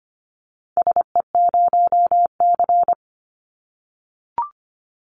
Morse code, heard as a keyed tone, about 25 wpm, 700 Hz. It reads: SE0C
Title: Call Signs - 25wpm